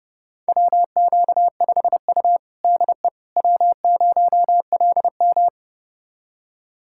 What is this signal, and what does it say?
Morse code, heard as a keyed tone, about 30 wpm, 700 Hz. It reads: WQ5U DE W0LM
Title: USA Callsign DE Callsign - 4 Characters - 30wpm